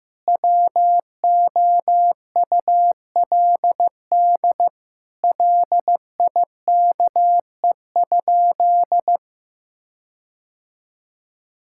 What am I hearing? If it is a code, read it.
Morse code, heard as a keyed tone, about 15 wpm, 700 Hz. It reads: WOULD LIKE?
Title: Sentences from Top 1600 Words 15wpm